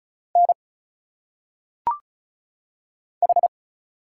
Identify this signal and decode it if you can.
Morse code, heard as a keyed tone, about 34 wpm, 700 Hz. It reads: N H
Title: Single Letters - Phonetic Pronunciation 35wpm